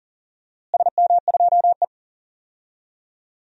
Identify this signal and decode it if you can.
Morse code, heard as a keyed tone, about 40 wpm, 700 Hz. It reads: SM2E
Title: DXCC-100 Call Signs - 4 Characters - Repeated 3x - 40wpm